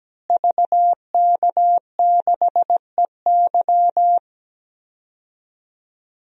Morse code, heard as a keyed tone, about 17 wpm, 700 Hz. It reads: VK6EY
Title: DXCC-100 Call Signs - 5 Characters - Repeated 2x - 17wpm